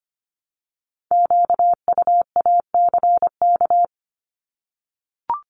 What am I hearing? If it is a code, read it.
Morse code, heard as a keyed tone, about 25 wpm, 700 Hz. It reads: QUACK
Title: Multi-letter Phonemes Repeated 3x 25wpm